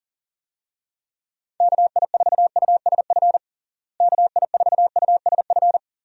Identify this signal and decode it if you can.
Morse code, heard as a keyed tone, about 37 wpm, 700 Hz. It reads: KI4USF KI4USF
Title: USA Call Signs - 6 Characters - Repeated 2x - 40wpm